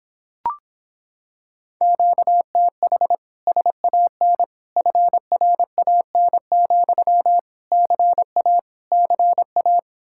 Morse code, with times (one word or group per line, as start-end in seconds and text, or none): 1.8-3.1 QTH
3.5-4.4 SAN
4.8-7.4 FRAN,
7.7-8.6 CA
8.9-9.8 CA